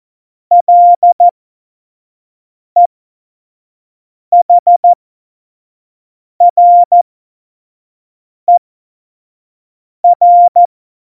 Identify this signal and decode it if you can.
Morse code, heard as a keyed tone, about 14 wpm, spaced as for 5 wpm, 700 Hz. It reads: LEHRER
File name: Morsekurs / Witze / Witz 30. Juli 2019, 25 F70